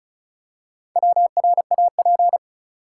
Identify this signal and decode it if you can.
Morse code, heard as a keyed tone, about 34 wpm, 700 Hz. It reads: WRAP
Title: Four Letter Words - 35wpm